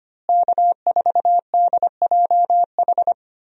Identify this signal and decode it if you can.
Morse code, heard as a keyed tone, about 25 wpm, 700 Hz. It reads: K4DJH
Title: USA Call Signs - 5 Characters - Repeated 2x - 25wpm